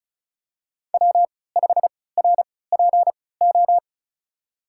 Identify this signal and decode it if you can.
Morse code, heard as a keyed tone, about 36 wpm, spaced as for 20 wpm, 700 Hz. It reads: W5RPO
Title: USA Call Signs - 5 Characters - 3x Character Spacing - 35wpm